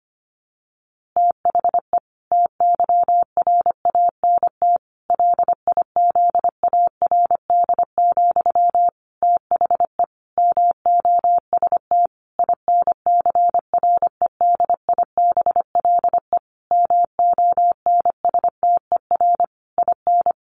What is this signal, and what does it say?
Morse code, heard as a keyed tone, about 25 wpm, 700 Hz. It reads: THE TYRANT LIZARD, THE MOST INCREDIBLE MONSTER IN
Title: A Sound Of Thunder 25wpm